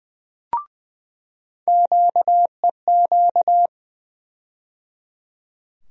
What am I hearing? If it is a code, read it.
Morse code, heard as a keyed tone, about 20 wpm, 700 Hz. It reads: QEQ
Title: USA Call Sign Postfixes - 3 Letters 20wpm